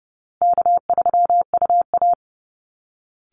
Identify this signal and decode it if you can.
Morse code, heard as a keyed tone, about 30 wpm, 700 Hz. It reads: K3UA
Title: Easy Call Signs Repeated 3x - 30wpm